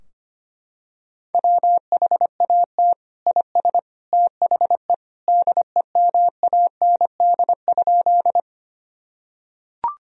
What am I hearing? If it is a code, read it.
Morse code, heard as a keyed tone, about 25 wpm, 700 Hz. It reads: WHAT IS THE DEMAND?
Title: Sentences from Top 1000 Words - Review Part 4 - 25wpm